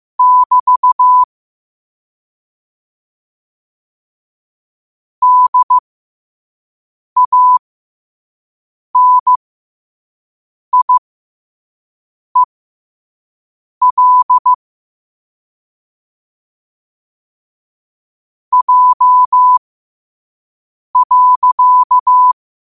Morse code, heard as a keyed tone, about 15 wpm, spaced as for 5 wpm, 1000 Hz. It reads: <BT> DANIEL J.
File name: qotdfiles / 5wpm